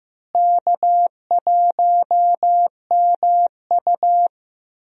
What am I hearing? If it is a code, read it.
Morse code, heard as a keyed tone, about 15 wpm, 700 Hz. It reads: K1MU
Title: USA Call Signs - 4 Characters - Repeated 3x - 15wpm